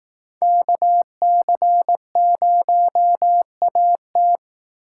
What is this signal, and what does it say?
Morse code, heard as a keyed tone, about 19 wpm, 700 Hz. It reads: KC0AT
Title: Speed Racing - USA 2x2 Call Signs 15wpm